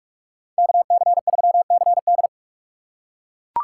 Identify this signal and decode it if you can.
Morse code, heard as a keyed tone, about 44 wpm, 700 Hz. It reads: KC3CD